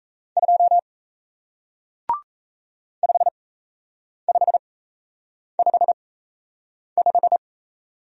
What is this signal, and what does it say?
Morse code, heard as a keyed tone, about 38 wpm, spaced as for 8 wpm, 700 Hz. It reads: 2 5555